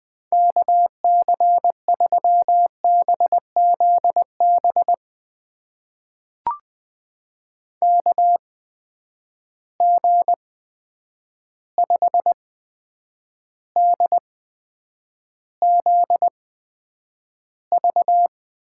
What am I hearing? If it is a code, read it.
Morse code, heard as a keyed tone, about 20 wpm, spaced as for 9 wpm, 700 Hz. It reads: KC3BZB KG5DZV
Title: USA Call Signs - 6 Characters - 8x Character Spacing - 20wpm